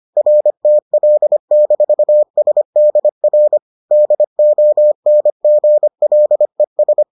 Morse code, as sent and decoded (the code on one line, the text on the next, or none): .-. - .-.. -....- ... -.. .-. / -.. --- -. --. .-.. ...
RTL-SDR DONGLES